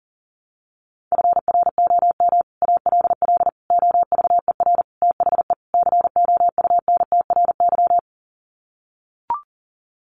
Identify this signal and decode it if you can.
Morse code, heard as a keyed tone, about 40 wpm, 700 Hz. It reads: FROM ALL OVER THE COUNTRY
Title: Sets of 5 Words Encore 40wpm